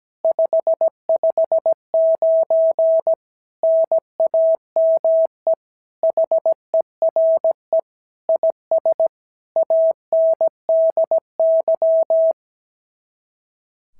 Morse code, heard as a keyed tone, about 16 wpm, 650 Hz. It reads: 559 NAME HERE IS ANDY
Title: SOTA QSO Practice 17wpm